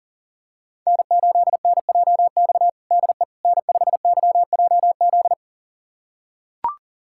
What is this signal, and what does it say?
Morse code, heard as a keyed tone, about 40 wpm, 700 Hz. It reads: N8NJX DE N5YJZ
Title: USA Callsign DE Callsign - 5 Characters - 40wpm